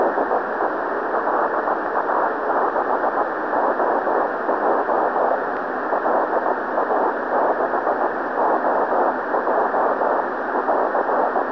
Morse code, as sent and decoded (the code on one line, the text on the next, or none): none